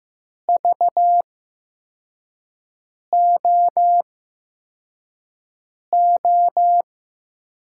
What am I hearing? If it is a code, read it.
Morse code, heard as a keyed tone, about 15 wpm, spaced as for 4 wpm, 700 Hz. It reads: VOO
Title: USA Call Signs - 6 Characters - 8x Character Spacing - 15wpm